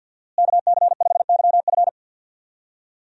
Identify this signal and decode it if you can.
Morse code, heard as a keyed tone, about 53 wpm, 700 Hz. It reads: KC5YF
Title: USA Call Signs - 5 Characters - Repeated 2x - 50wpm